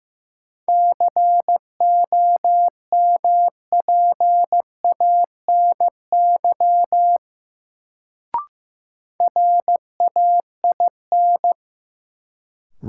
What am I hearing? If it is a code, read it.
Morse code, heard as a keyed tone, about 15 wpm, 700 Hz. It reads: COMPANY RAIN
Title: Sentences from Top 600 Words - New Word Drill - 15wpm